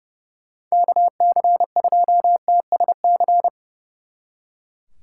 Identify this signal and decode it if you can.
Morse code, heard as a keyed tone, about 40 wpm, 700 Hz. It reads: KC2TSC